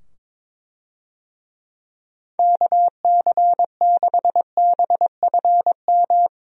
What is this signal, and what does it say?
Morse code, heard as a keyed tone, about 22 wpm, 700 Hz. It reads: KC6BFM
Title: USA Call Signs - 6 Characters - 22wpm